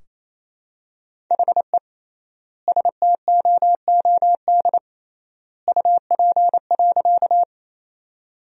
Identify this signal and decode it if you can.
Morse code, heard as a keyed tone, about 28 wpm, 700 Hz. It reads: HE STOOD UP.
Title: 3x Word-Spacing - Sentences from Top 500 Words - 28wpm